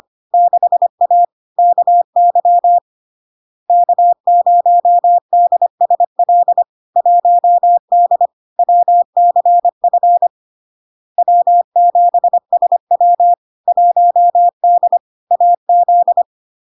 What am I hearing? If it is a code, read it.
Morse code, heard as a keyed tone, about 25 wpm, 700 Hz. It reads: 6A KY K0DSL 1D WCF W7SW 1D AZ
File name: LARK-FD-2022-Log-in-CW-25wpm-700-Hz